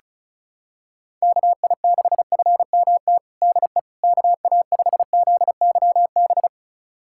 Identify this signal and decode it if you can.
Morse code, heard as a keyed tone, about 34 wpm, 700 Hz. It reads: KI6FMT DE KA5ZYB
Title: USA Callsign DE Callsign - 6 Characters - 35wpm